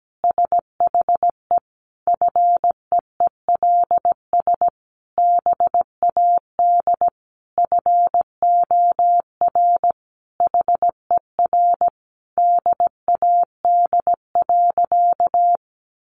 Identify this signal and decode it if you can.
Morse code, heard as a keyed tone, about 17 wpm, 700 Hz. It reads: SHE FEELS BAD FOR HER DAD.
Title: Additional Easy Phrases 17wpm